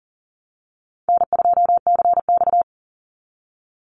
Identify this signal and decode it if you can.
Morse code, heard as a keyed tone, about 40 wpm, 700 Hz. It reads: N2CX
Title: Easy Call Signs Repeated 2x - 40wpm